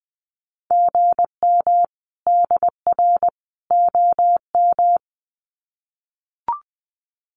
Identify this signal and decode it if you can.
Morse code, heard as a keyed tone, about 20 wpm, 700 Hz. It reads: GM DR OM
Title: Common QSO Phrases 20wpm